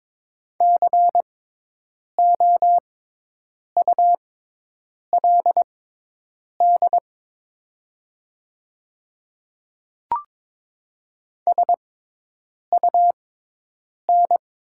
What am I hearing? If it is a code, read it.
Morse code, heard as a keyed tone, about 22 wpm, spaced as for 8 wpm, 700 Hz. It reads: COULD SUN
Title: 6x Character Spacing - Top 300 Words - 22wpm